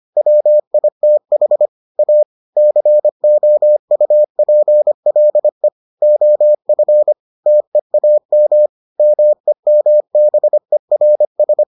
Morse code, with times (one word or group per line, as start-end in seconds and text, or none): 0.2-1.7 WITH
2.0-2.2 A
2.6-5.7 COUPLE
6.0-7.1 OF
7.5-8.7 TEAM
9.0-11.6 MEMBERS